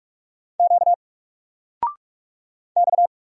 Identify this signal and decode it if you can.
Morse code, heard as a keyed tone, about 46 wpm, 700 Hz. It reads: Q X
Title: Single Letter-Number 45wpm